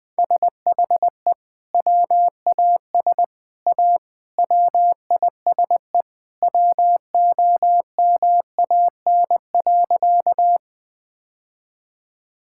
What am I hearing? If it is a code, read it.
Morse code, heard as a keyed tone, about 20 wpm, 700 Hz. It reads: SHE WAS A WISE WOMAN.
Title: Sentences from Top 1600 Words 20wpm